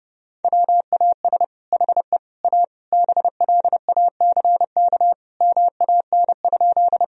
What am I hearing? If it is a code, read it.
Morse code, heard as a keyed tone, about 30 wpm, 700 Hz. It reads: WAS HE A BLACK MAN?
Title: Sentences from Top 1000 Words - Review Part 2 - 30wpm